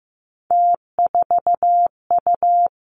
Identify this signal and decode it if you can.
Morse code, heard as a keyed tone, about 15 wpm, 700 Hz. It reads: T4U
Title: Three Challenging Letter-Number Combinations 15wpm